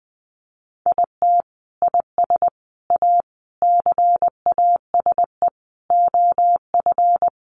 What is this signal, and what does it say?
Morse code, heard as a keyed tone, about 19 wpm, 700 Hz. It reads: IT IS A CASE OF
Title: Sentences from Top 1400 Words 20wpm